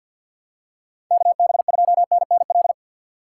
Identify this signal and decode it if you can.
Morse code, heard as a keyed tone, about 47 wpm, 700 Hz. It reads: KB2NNL
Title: USA Call Signs - 6 Characters - 4x Character Spacing - 50wpm